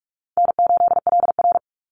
Morse code, heard as a keyed tone, about 44 wpm, 700 Hz. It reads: N8LR
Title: Easy Call Signs Repeated 3x - 45wpm